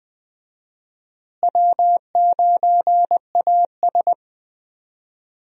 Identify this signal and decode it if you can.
Morse code, heard as a keyed tone, about 21 wpm, 700 Hz. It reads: W9AS
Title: USA Call Signs - 4 Characters - Repeated 3x - 20wpm